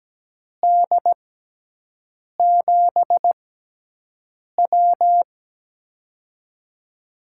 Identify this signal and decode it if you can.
Morse code, heard as a keyed tone, about 17 wpm, spaced as for 6 wpm, 700 Hz. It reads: D7W
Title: USA Call Signs - 4 Characters - 6x Character Spacing - 17wpm